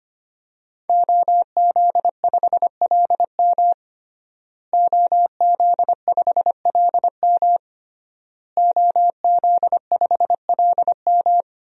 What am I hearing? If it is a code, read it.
Morse code, heard as a keyed tone, about 25 wpm, 700 Hz. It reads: OZ5LM OZ5LM OZ5LM
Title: DXCC-100 Call Signs - 5 Characters - Repeated 3x - 25wpm